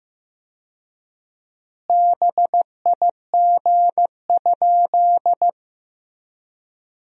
Morse code, heard as a keyed tone, about 15 wpm, 700 Hz. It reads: BIG?
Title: 4x Word-Spacing - Sentences from Top 200 Words - 15wpm